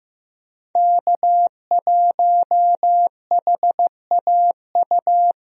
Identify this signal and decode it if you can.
Morse code, heard as a keyed tone, about 15 wpm, 700 Hz. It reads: K1HAU
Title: USA Call Signs - 5 Characters - 15wpm